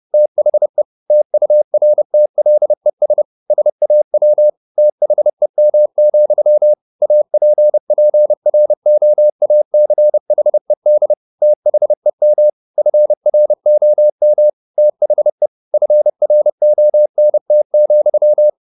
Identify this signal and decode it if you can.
Morse code, heard as a keyed tone, about 30 wpm, 600 Hz. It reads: THE TURTLES SAW THEM, APPROACHED THEM FROM THE FRONT,
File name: the old man and the sea / the old man and the sea 1 30wpm